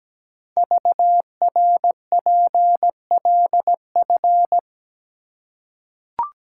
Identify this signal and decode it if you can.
Morse code, heard as a keyed tone, about 17 wpm, 700 Hz. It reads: VRPLF
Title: Code Groups - 5 Letters - 17wpm